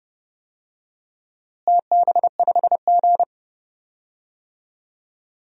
Morse code, heard as a keyed tone, about 30 wpm, 700 Hz. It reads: TB5G
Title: DXCC-100 Call Signs - 4 Characters - Repeated 2x - 30wpm